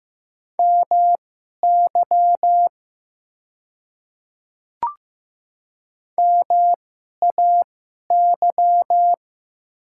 Morse code, heard as a keyed tone, about 15 wpm, spaced as for 11 wpm, 690 Hz. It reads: MY MAY
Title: 2x Character Spacing - Top 100 Words - 15wpm